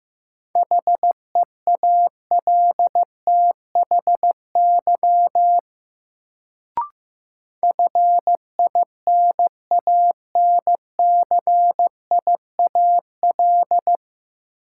Morse code, heard as a keyed tone, about 15 wpm, 700 Hz. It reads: HEALTHY FINANCIAL
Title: Sentences from Top 1000 Words - Review Part 5 - New Word Drill - 15wpm